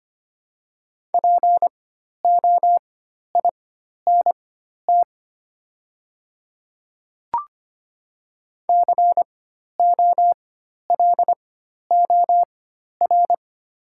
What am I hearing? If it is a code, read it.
Morse code, heard as a keyed tone, about 25 wpm, spaced as for 12 wpm, 700 Hz. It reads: POINT COLOR
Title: 4x Character Spacing - Top 500 Words - 25wpm